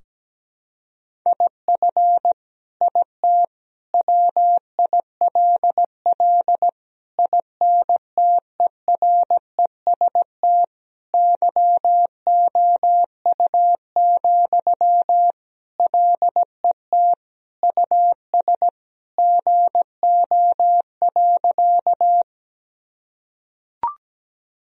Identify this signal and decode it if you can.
Morse code, heard as a keyed tone, about 17 wpm, 700 Hz. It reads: IF IT WILL INTEREST YOU, LET US GO.
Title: Sentences from Top 1000 Words - Review Part 2 - 17wpm